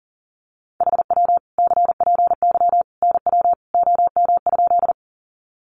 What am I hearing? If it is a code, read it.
Morse code, heard as a keyed tone, about 40 wpm, 700 Hz. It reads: HW CPY NW OM?
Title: Common QSO Phrases 40wpm